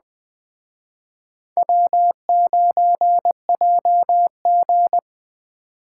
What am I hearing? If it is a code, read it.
Morse code, heard as a keyed tone, about 20 wpm, 700 Hz. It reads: W9JG